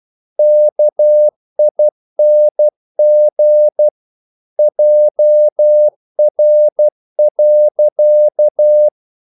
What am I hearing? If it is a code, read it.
Morse code, heard as a keyed tone, about 12 wpm, 600 Hz. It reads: KING JR.
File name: CW / Practice / EnglishQuotations0112